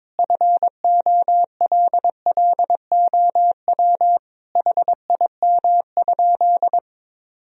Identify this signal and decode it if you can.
Morse code, heard as a keyed tone, about 22 wpm, 700 Hz. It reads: FOLLOW HIM?
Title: Sentences from Top 1000 Words - Review Part 1 - 22wpm